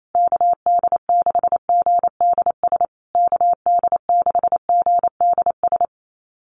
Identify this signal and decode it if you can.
Morse code, heard as a keyed tone, about 28 wpm, 700 Hz. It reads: KD6GDS KD6GDS
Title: USA Call Signs Repeated 3x 28wpm